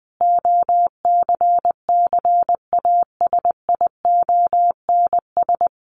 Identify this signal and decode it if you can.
Morse code, heard as a keyed tone, about 20 wpm, 700 Hz. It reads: OCCASIONS
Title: A Sound Of Thunder 20wpm